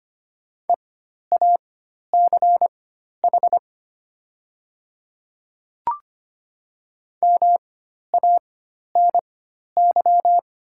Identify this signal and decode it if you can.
Morse code, heard as a keyed tone, about 25 wpm, spaced as for 12 wpm, 700 Hz. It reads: EACH MANY
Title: 4x Character Spacing - Top 200 Words - 25wpm